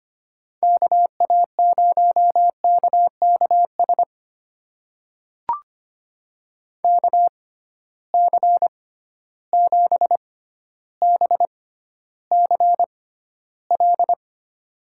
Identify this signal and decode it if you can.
Morse code, heard as a keyed tone, about 25 wpm, spaced as for 13 wpm, 700 Hz. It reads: KA0KKS KC7BCL
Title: USA Call Signs - 6x Character Spacing - 25wpm